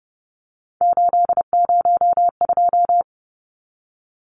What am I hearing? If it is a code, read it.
Morse code, heard as a keyed tone, about 30 wpm, 700 Hz. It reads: 802
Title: Numbers - 3-Digit 30wpm